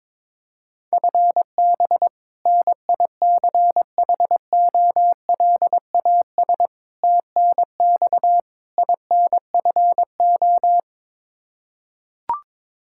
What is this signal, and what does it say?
Morse code, heard as a keyed tone, about 22 wpm, 700 Hz. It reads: FB NICHOLAS TNX INFO
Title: K1USN SST QSO Practice 22wpm